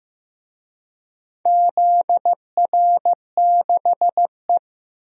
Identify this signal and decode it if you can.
Morse code, heard as a keyed tone, about 15 wpm, 700 Hz. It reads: ZR6E
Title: DXCC-100 Call Signs - 4 Characters - Repeated 3x - 15wpm